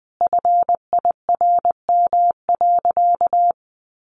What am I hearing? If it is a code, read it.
Morse code, heard as a keyed tone, about 20 wpm, 700 Hz. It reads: FIRM.
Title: Sentences from Top 1400 Words 20wpm